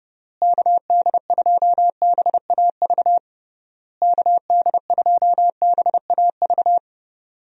Morse code, heard as a keyed tone, about 30 wpm, 700 Hz. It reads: KD2BAV KD2BAV
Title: USA Call Signs - 6 Characters - Repeated 3x - 30wpm